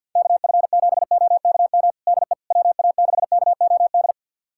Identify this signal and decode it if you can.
Morse code, heard as a keyed tone, about 48 wpm, 690 Hz. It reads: KF7OKM DE WA6KOD
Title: USA Callsign DE Callsign - 6 Characters - 50wpm